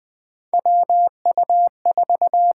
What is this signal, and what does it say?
Morse code, heard as a keyed tone, about 20 wpm, 700 Hz. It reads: WU4
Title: USA Call Signs - 4 Characters - 8x Character Spacing - 20wpm